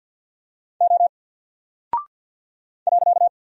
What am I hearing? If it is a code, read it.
Morse code, heard as a keyed tone, about 48 wpm, 700 Hz. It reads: O .
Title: Single Character (Letter:Number:Punctuation:Prosign) Alt 50wpm